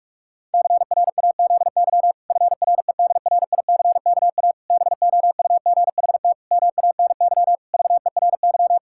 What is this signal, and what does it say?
Morse code, heard as a keyed tone, about 41 wpm, 700 Hz. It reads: CRAZY FREDRICKA BOUGHT MANY VERY
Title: Pangram Challenge v2 - 45wpm